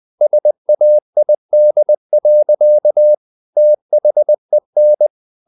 Morse code, heard as a keyed tone, about 20 wpm, 600 Hz. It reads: SAID. THEN
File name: the old man and the sea / the old man and the sea part 1 20wpm